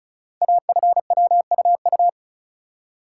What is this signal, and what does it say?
Morse code, heard as a keyed tone, about 36 wpm, 690 Hz. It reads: AFWUU
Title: Code Groups - 5 Letters - 35wpm